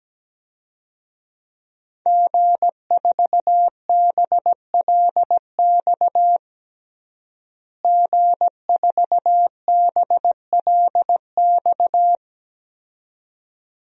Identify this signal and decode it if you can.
Morse code, heard as a keyed tone, about 17 wpm, 700 Hz. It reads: G4BLX G4BLX
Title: DXCC-100 Call Signs - 5 Characters - Repeated 2x - 17wpm